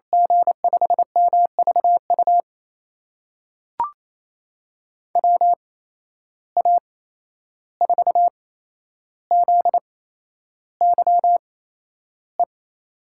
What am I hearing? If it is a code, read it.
Morse code, heard as a keyed tone, about 28 wpm, spaced as for 11 wpm, 700 Hz. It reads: G5MVU WA4ZYE
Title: USA Call Signs - 6 Characters - 8x Character Spacing - 28wpm